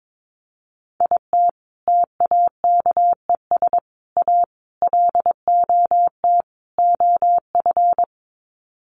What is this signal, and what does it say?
Morse code, heard as a keyed tone, about 22 wpm, 700 Hz. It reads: IT TAKES A LOT OF
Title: Sets of 5 Words Encore 22wpm